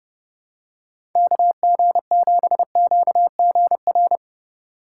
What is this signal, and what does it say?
Morse code, heard as a keyed tone, about 31 wpm, 700 Hz. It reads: KG7QGR
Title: USA Call Signs - 6 Characters - 2x Character Spacing - 30wpm